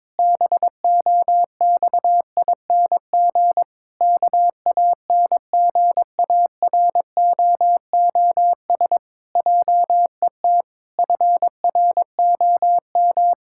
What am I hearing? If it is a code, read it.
Morse code, heard as a keyed tone, about 22 wpm, 700 Hz. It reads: BOXING KANGAROOS JET FROM
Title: Pangram Challenge v2 - 22wpm